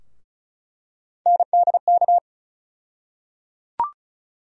Morse code, heard as a keyed tone, about 36 wpm, 700 Hz. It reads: NDK
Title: USA Call Sign Postfixes - 3 Letters 35wpm